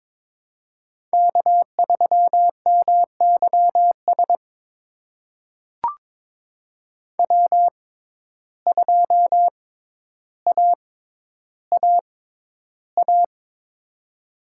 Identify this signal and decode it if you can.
Morse code, heard as a keyed tone, about 23 wpm, spaced as for 11 wpm, 700 Hz. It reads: K3MYS W2AAA
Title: USA Call Signs - 5 Characters - 6x Character Spacing - 22wpm